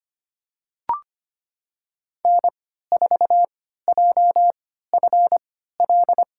Morse code, heard as a keyed tone, about 25 wpm, spaced as for 14 wpm, 700 Hz. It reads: N4JFL
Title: USA Call Signs - 5 Characters - 3x Character Spacing - 25wpm